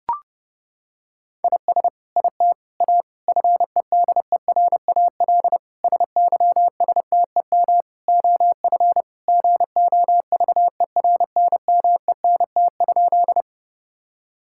Morse code, heard as a keyed tone, about 32 wpm, 700 Hz. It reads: IS IT A FEDERAL SYSTEM OF GOVERNMENT?
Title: Sentences from Top 1000 Words - Review Part 5 - 30wpm